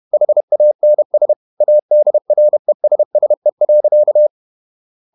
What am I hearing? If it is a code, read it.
Morse code, heard as a keyed tone, about 31 wpm, 600 Hz. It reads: HANS ADRESSE.
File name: Ferie QRQ 31wpm